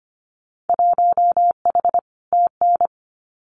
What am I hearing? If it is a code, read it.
Morse code, heard as a keyed tone, about 24 wpm, 700 Hz. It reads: 1H TN
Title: Winter Field Day Exchanges 25wpm